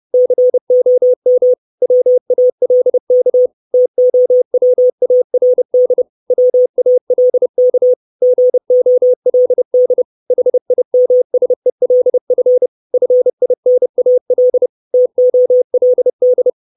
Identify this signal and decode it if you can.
Morse code, heard as a keyed tone, about 30 wpm, 500 Hz. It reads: COM WALK TOWARD WALK GOLD HIMSELF FINAL TOLD
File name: 632 Words 30WPM